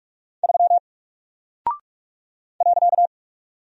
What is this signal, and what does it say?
Morse code, heard as a keyed tone, about 44 wpm, 700 Hz. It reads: 3 .
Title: Single Character (Letter:Number:Punctuation:Prosign) Alt 45wpm